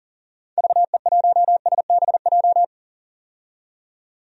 Morse code, heard as a keyed tone, about 40 wpm, 700 Hz. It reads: VE1SBJ
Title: DXCC-100 Call Signs - 6 Characters - 40wpm